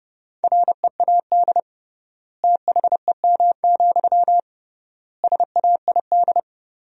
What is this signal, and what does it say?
Morse code, heard as a keyed tone, about 30 wpm, 700 Hz. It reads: READ THEM, SAID
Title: 3x Word-Spacing - Sentences from Top 400 Words - 30wpm